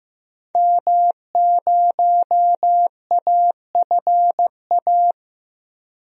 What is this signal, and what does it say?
Morse code, heard as a keyed tone, about 16 wpm, 700 Hz. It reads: M0AFA
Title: DXCC-100 Call Signs - 5 Characters - Repeated 2x - 15wpm